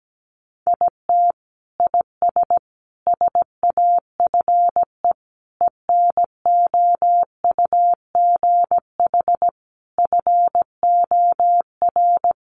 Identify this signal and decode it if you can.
Morse code, heard as a keyed tone, about 17 wpm, 700 Hz. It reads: IT IS SAFE ENOUGH FOR
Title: Sentences from Top 900 Words 17wpm